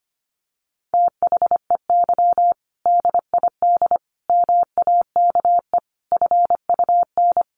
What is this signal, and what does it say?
Morse code, heard as a keyed tone, about 25 wpm, 700 Hz. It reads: THEY DID MAKE FUN
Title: Sentences from Top 1300 Words 25wpm